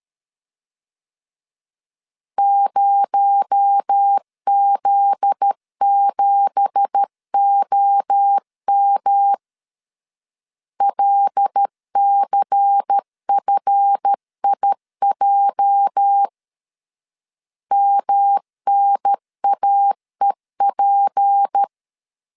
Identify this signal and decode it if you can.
Morse code, heard as a keyed tone, about 13 wpm, 800 Hz. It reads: 0Z7OM LCFIJ MNAEP
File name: CW / DET / 1 9 1500C S B 12wpm 3819